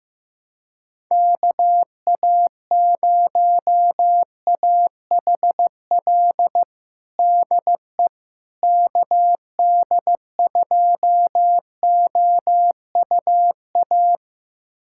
Morse code, heard as a keyed tone, about 15 wpm, 700 Hz. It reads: KA0AHL DE KD2OUA
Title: USA Callsign DE Callsign 15wpm